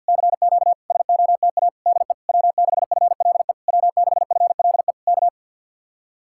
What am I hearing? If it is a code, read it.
Morse code, heard as a keyed tone, about 64 wpm, 700 Hz. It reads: CQ SOTA DE W6FLE W6FLE K